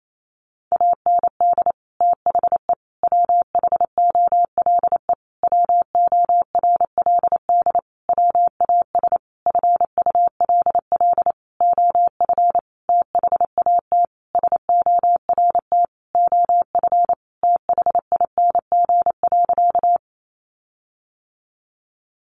Morse code, heard as a keyed tone, about 28 wpm, 700 Hz. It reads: AND THE WHOLE WORLD WAS FULL OF THAT SORT OF THING.
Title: Sentences from Top 1300 Words 28wpm